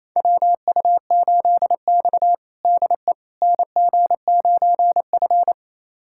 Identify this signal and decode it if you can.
Morse code, heard as a keyed tone, about 28 wpm, 700 Hz. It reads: WU8X DE NG9F
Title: USA Callsign DE Callsign - 4 Characters - 28wpm